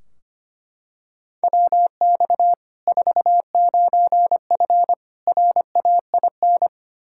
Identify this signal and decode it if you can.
Morse code, heard as a keyed tone, about 25 wpm, 700 Hz. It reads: WX 49F RAIN
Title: Practice QSO - Exchange 2 - Info - 25wpm